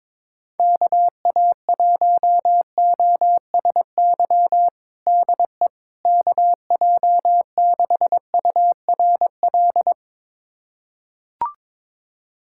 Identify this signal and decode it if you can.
Morse code, heard as a keyed tone, about 22 wpm, 700 Hz. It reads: KA1OSY DE KJ6URL
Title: USA Callsign DE Callsign - 6 Characters - 22wpm